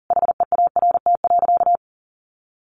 Morse code, heard as a keyed tone, about 40 wpm, 700 Hz. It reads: HEART.
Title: Sentences from Top 1200 Words 40wpm